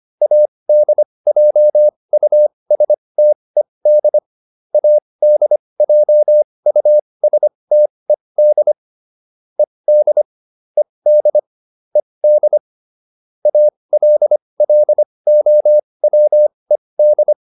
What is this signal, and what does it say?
Morse code, heard as a keyed tone, about 25 wpm, spaced as for 20 wpm, 600 Hz. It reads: ADJUSTED ADJUSTED ED ED ED ALLOWED
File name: ED8 20